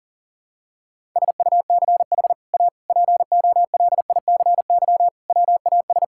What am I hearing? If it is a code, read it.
Morse code, heard as a keyed tone, about 40 wpm, 700 Hz. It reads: SUCH A POLICY WAS
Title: Sentences from Top 1000 Words - Review Part 5 - 40wpm